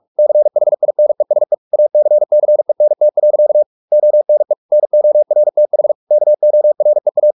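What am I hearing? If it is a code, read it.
Morse code, heard as a keyed tone, about 44 wpm, 600 Hz. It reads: CHINESE ACCENT. ONE NORTH KOREA
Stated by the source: Morse practice at 45 wpm, the novel Never by Ken Follett in English, from CW med Gnister.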